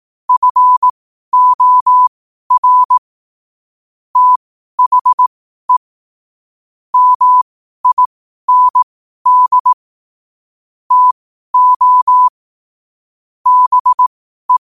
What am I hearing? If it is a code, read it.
Morse code, heard as a keyed tone, about 18 wpm, spaced as for 12 wpm, 1000 Hz. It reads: FOR THE MIND TO BE
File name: qotdfiles / 12wpm